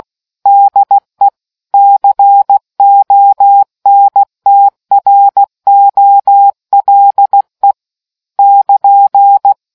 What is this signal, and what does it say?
Morse code, heard as a keyed tone, about 16 wpm, 800 Hz. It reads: DE CONTROLE <KN>